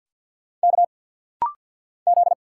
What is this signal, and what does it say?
Morse code, heard as a keyed tone, about 50 wpm, 700 Hz. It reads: K Z
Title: Single Letter-Number Alt - Rapid-Fire 50wpm